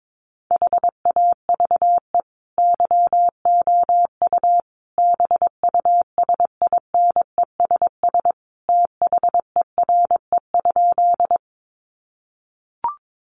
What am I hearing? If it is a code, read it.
Morse code, heard as a keyed tone, about 22 wpm, 700 Hz. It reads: HAVE YOU BUSINESS THERE?
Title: Sentences from Top 600 Words 22wpm